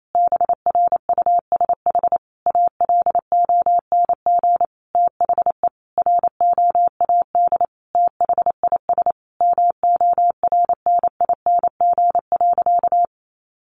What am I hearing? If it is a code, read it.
Morse code, heard as a keyed tone, about 28 wpm, 700 Hz. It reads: BRUSH ALONG THE ROAD THIS MORNING.
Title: Sentences from Top 1100 Words 28wpm